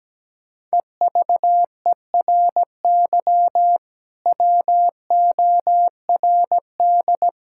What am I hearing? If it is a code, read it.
Morse code, heard as a keyed tone, about 17 wpm, 700 Hz. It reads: EVERY WORD
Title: Sentences from Top 1600 Words 17wpm